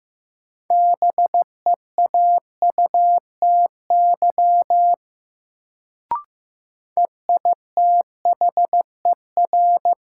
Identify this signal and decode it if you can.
Morse code, heard as a keyed tone, about 15 wpm, 700 Hz. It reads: BEAUTY EITHER
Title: Sentences from Top 600 Words - New Word Drill - 15wpm